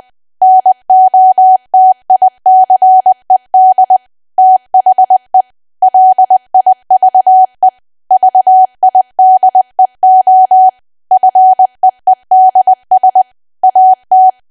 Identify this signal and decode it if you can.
Morse code, heard as a keyed tone, about 20 wpm, 750 Hz. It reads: NOTICED THE LIVE VIDEO FEEDS AT
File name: Archive / 20 WPM / 20WPM